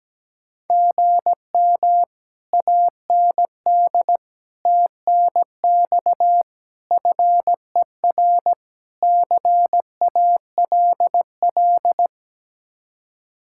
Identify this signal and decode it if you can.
Morse code, heard as a keyed tone, about 17 wpm, 700 Hz. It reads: GM AND TNX FER CALL
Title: Speed-Racing Common QSO Phrases - 17wpm